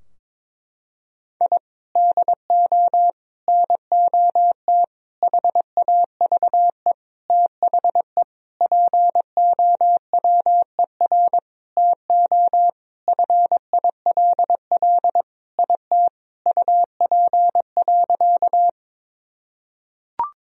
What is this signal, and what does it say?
Morse code, heard as a keyed tone, about 22 wpm, 700 Hz. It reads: I DO NOT HAVE THE POWER TO FILL IT UP.
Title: Sentences from Top 1000 Words - Review Part 3 - 22wpm